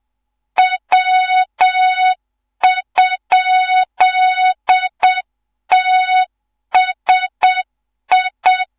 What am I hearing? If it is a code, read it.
Morse code, heard as a keyed tone, about 7 wpm, 3700 Hz, with noise. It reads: W?TSI
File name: Morze-HA2MN / hang / 14 280 HA2MN